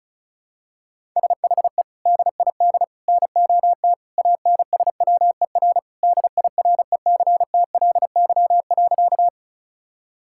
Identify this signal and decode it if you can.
Morse code, heard as a keyed tone, about 36 wpm, 700 Hz. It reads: SHE DID NOT ANSWER DIRECTLY.
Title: Sentences from Top 1000 Words - Review Part 5 - 35wpm